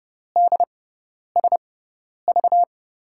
Morse code, heard as a keyed tone, about 29 wpm, spaced as for 10 wpm, 700 Hz. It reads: DSV
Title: USA Call Signs - 6x Character Spacing - 30wpm